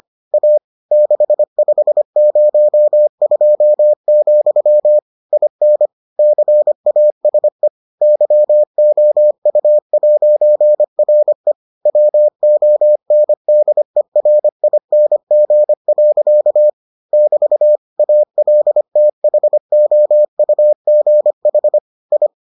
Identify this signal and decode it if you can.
Morse code, heard as a keyed tone, about 25 wpm, 600 Hz. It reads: A 6502, IN CASE YOU'RE WONDERING. <BT> ALTHOUGH I